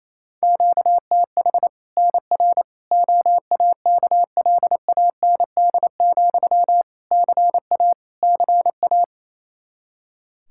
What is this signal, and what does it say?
Morse code, heard as a keyed tone, about 27 wpm, 700 Hz. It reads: QTH NR OAKLAND, CA CA
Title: K1USN SST QSO Practice 28wpm